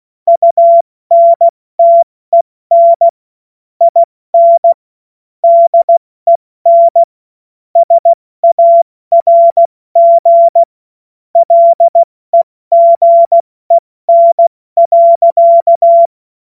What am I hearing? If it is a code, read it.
Morse code, heard as a keyed tone, about 16 wpm, 680 Hz. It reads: UNTEN IN DEN SARG LEGEN.